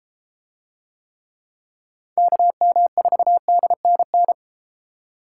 Morse code, heard as a keyed tone, about 32 wpm, 700 Hz. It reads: KM4DNN